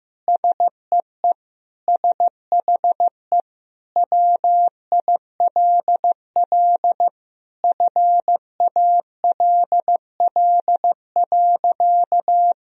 Morse code, heard as a keyed tone, about 15 wpm, 700 Hz. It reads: SEE SHE WILL FALL.
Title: Sentences from Top 1000 Words - Review Part 2 - 15wpm